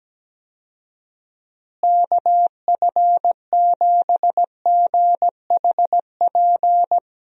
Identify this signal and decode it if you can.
Morse code, heard as a keyed tone, about 17 wpm, 700 Hz. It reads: KF7GHP
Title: USA Call Signs - 6 Characters - Repeated 2x - 17wpm